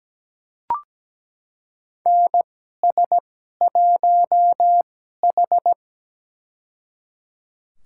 Morse code, heard as a keyed tone, about 17 wpm, spaced as for 12 wpm, 700 Hz. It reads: NS1H
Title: USA Call Signs - 4 Characters - 2x Character Spacing - 17wpm